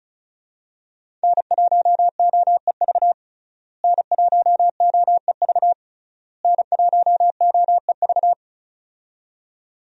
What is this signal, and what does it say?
Morse code, heard as a keyed tone, about 35 wpm, 700 Hz. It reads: N1OEV N1OEV N1OEV
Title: USA Call Signs - 5 Characters - Repeated 3x - 35wpm